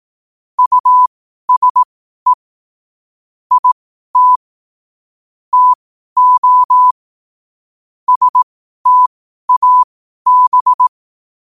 Morse code, heard as a keyed tone, about 18 wpm, spaced as for 12 wpm, 1000 Hz. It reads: USE IT TO STAB
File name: qotdfiles / 12wpm